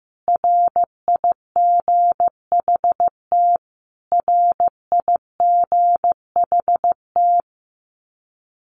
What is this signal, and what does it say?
Morse code, heard as a keyed tone, about 15 wpm, 700 Hz. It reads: RIGHT RIGHT
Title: Top 500 Words Repeated 15wpm